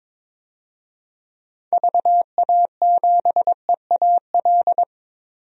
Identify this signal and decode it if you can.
Morse code, heard as a keyed tone, about 22 wpm, 700 Hz. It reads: VA7EAL
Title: DXCC-100 Call Signs - 6 Characters - Repeated 3x - 22wpm